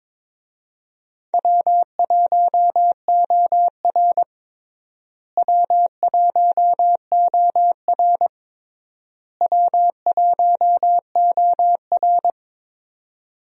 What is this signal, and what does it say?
Morse code, heard as a keyed tone, about 22 wpm, 700 Hz. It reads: W1OR W1OR W1OR
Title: USA Call Signs - 4 Characters - Repeated 3x - 22wpm